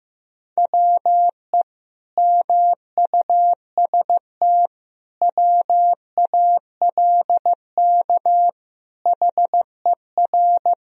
Morse code, heard as a keyed tone, about 15 wpm, 700 Hz. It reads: WE MUST WALK HER
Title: Sentences from Top 1000 Words - Review Part 2 - 15wpm